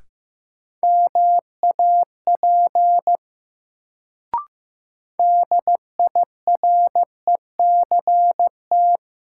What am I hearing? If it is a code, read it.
Morse code, heard as a keyed tone, about 15 wpm, 700 Hz. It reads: MAP DIRECT
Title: Sentences from Top 400 Words - New Word Drill - 15wpm